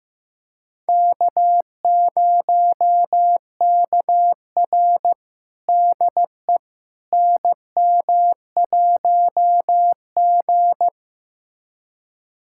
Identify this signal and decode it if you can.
Morse code, heard as a keyed tone, about 15 wpm, 700 Hz. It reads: K0KR DE NM1G
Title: USA Callsign DE Callsign - 4 Characters - 15wpm